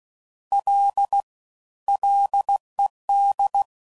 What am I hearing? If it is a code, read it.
Morse code, heard as a keyed tone, about 16 wpm, 800 Hz. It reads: L LED
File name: Prefixes / FLs & THs / Single Fs Ls